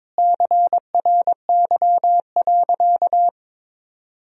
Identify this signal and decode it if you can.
Morse code, heard as a keyed tone, about 22 wpm, 700 Hz. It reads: CRY.
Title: Sentences from Top 500 Words and Abbreviations 22wpm